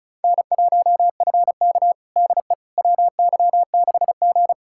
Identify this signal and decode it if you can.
Morse code, heard as a keyed tone, about 34 wpm, 700 Hz. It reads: N1FK DE WY6G
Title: USA Callsign DE Callsign - 4 Characters - 35wpm